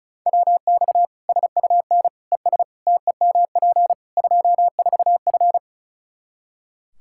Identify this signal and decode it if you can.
Morse code, heard as a keyed tone, about 36 wpm, 700 Hz. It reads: WX SUN ES TEMP 24F
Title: Practice QSO - Exchange 2 - Info - 35wpm